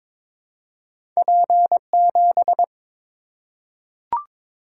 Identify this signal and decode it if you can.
Morse code, heard as a keyed tone, about 23 wpm, 700 Hz. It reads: P7
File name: practice / 22wpm / Two Letter-Number Combinations - 22wpm